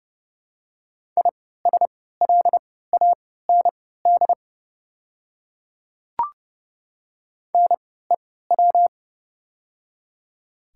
Morse code, heard as a keyed tone, about 30 wpm, spaced as for 17 wpm, 700 Hz. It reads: ISLAND NEW
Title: 3x Character Spacing - Top 500 Words - 30wpm